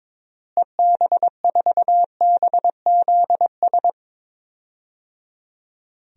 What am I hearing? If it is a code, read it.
Morse code, heard as a keyed tone, about 22 wpm, 700 Hz. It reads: EB4BZS
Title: DXCC-100 Call Signs - 6 Characters - Repeated 3x - 22wpm